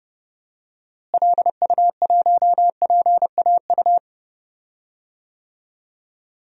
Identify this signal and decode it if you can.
Morse code, heard as a keyed tone, about 30 wpm, 700 Hz. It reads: LU1PAU
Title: DXCC-100 Call Signs - 6 Characters - Repeated 2x - 30wpm